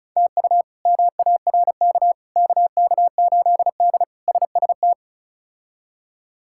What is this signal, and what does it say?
Morse code, heard as a keyed tone, about 34 wpm, 700 Hz. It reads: TU MARK KK8D SST
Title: K1USN SST QSO Practice - 35wpm